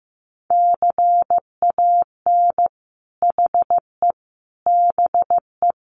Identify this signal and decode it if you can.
Morse code, heard as a keyed tone, about 15 wpm, 700 Hz. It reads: CAN HE BE
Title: Sentences from Top 1500 Words 15wpm